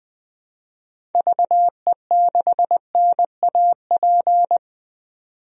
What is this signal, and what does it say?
Morse code, heard as a keyed tone, about 20 wpm, 700 Hz. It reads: VE6NAP
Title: DXCC-100 Call Signs - 6 Characters - 20wpm